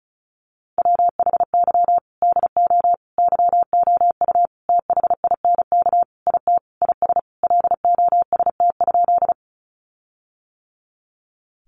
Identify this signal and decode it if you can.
Morse code, heard as a keyed tone, about 34 wpm, 700 Hz. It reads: WHY DO YOU THINK IT IS LOST?
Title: Sentences from Top 1400 Words 35wpm